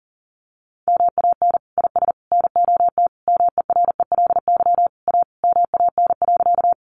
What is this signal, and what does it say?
Morse code, heard as a keyed tone, about 40 wpm, 700 Hz. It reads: MAN IS NOT MERELY A MAN.
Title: Sentences from Top 1400 Words 40wpm